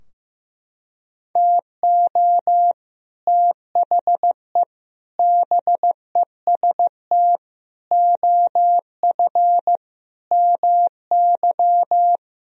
Morse code, heard as a keyed tone, about 15 wpm, 700 Hz. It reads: TO THE BEST OF MY
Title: Sentences from Top 1000 Words - Review Part 3 - 15wpm